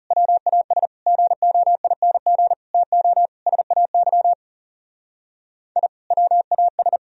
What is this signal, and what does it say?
Morse code, heard as a keyed tone, about 40 wpm, 700 Hz. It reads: WAS GOING TO SAY I WAS